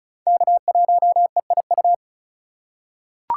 Morse code, heard as a keyed tone, about 36 wpm, 700 Hz. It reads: K1EIU
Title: USA Call Signs - 5 Characters - 3x Character Spacing - 35wpm